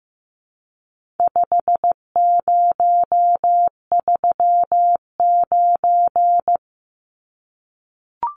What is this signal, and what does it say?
Morse code, heard as a keyed tone, about 14 wpm, 700 Hz. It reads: 5039
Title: Numbers - 4-Digit 15wpm